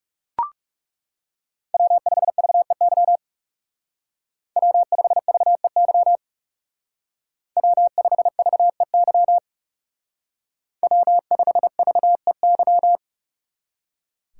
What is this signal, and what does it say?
Morse code, heard as a keyed tone, about 34 wpm, 700 Hz. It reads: W5VEY W5VEY W5VEY W5VEY
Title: Speed Racing - USA 1x3 Call Signs 30wpm